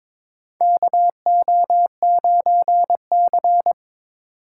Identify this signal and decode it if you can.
Morse code, heard as a keyed tone, about 21 wpm, 700 Hz. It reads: KO9C
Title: USA Call Signs - 4 Characters - 2x Character Spacing - 22wpm